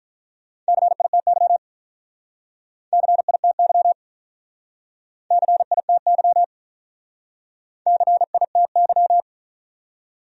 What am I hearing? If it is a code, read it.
Morse code, heard as a keyed tone, about 44 wpm, 700 Hz. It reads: CITY CITY CITY CITY